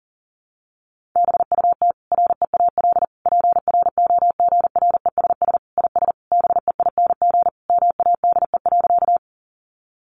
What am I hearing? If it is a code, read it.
Morse code, heard as a keyed tone, about 40 wpm, 700 Hz. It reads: BUT REAL PROGRESS IS BEING MADE.
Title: Sentences from Top 1400 Words 40wpm